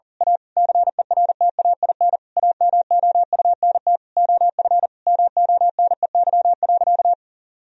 Morse code, heard as a keyed tone, about 40 wpm, 700 Hz. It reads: A CERTAIN AMOUNT OF MONEY.
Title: Sentences from Top 1000 Words - Review Part 4 - 40wpm